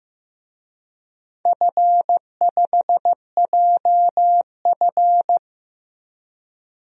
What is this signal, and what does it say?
Morse code, heard as a keyed tone, about 15 wpm, 700 Hz. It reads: F5JF
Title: DXCC-100 Call Signs - 4 Characters - Repeated 3x - 15wpm